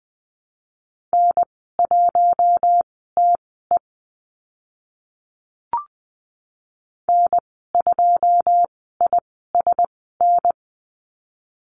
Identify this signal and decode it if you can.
Morse code, heard as a keyed tone, about 20 wpm, spaced as for 14 wpm, 700 Hz. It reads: N1TE N2ISN
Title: 082 Lesson 14g - Number 2 - Call Signs 20wpm